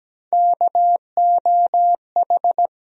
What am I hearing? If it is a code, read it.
Morse code, heard as a keyed tone, about 17 wpm, 700 Hz. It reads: KOH
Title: USA Call Sign Postfixes - 3 Letters 17wpm